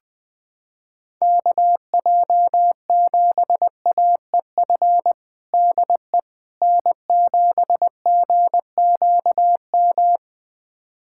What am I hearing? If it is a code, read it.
Morse code, heard as a keyed tone, about 19 wpm, 700 Hz. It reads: KJ7AEF DE N7GQM
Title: USA Callsign DE Callsign 20wpm